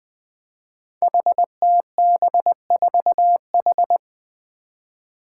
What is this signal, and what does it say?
Morse code, heard as a keyed tone, about 21 wpm, 700 Hz. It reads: HTB4H
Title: Five Challenging Letter-Number Combinations 20wpm